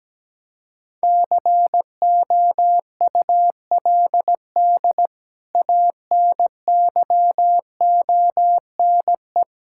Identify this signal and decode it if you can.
Morse code, heard as a keyed tone, about 17 wpm, 700 Hz. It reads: COULD ANYONE
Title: Sentences from Top 1600 Words 17wpm